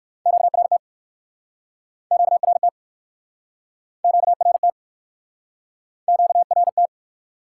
Morse code, heard as a keyed tone, about 60 wpm, 700 Hz. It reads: QRT QRT QRT QRT